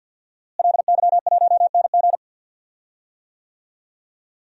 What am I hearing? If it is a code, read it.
Morse code, heard as a keyed tone, about 52 wpm, 700 Hz. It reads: LY1NG
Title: DXCC-100 Call Signs - 5 Characters - Repeated 3x - 50wpm